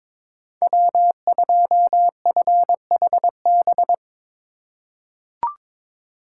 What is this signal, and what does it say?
Morse code, heard as a keyed tone, about 22 wpm, 700 Hz. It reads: W2FHB
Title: USA Call Signs - 5 Characters - 4x Character Spacing - 22wpm